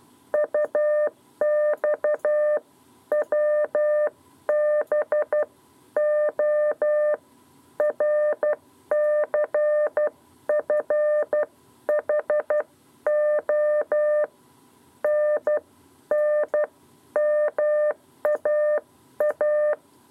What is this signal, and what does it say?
Morse code, heard as a keyed tone, about 12 wpm, 600 Hz. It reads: UX WB O RCFHO N N MAA
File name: 7b lezione cw 2